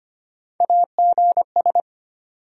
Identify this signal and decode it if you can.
Morse code, heard as a keyed tone, about 25 wpm, 700 Hz. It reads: AGS
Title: USA Call Sign Postfixes - 3 Letters 25wpm